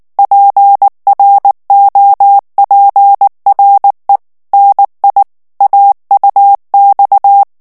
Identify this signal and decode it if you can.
Morse code, heard as a keyed tone, about 19 wpm, 800 Hz. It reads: PROPRE NI AUX
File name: FRANCAIS TEXTE CLAIR 2 VITESSE 18 MOTS MINUTES